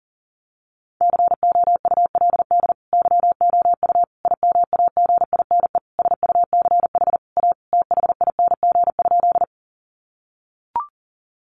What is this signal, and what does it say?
Morse code, heard as a keyed tone, about 37 wpm, 700 Hz. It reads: COULD YOU IMAGINE SUCH A THING?
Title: Sentences from Top 1400 Words 40wpm